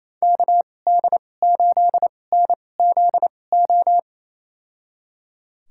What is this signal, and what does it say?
Morse code, heard as a keyed tone, about 28 wpm, spaced as for 20 wpm, 700 Hz. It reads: KD8NZO
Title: USA Call Signs - 6 Characters - 2x Character Spacing - 28wpm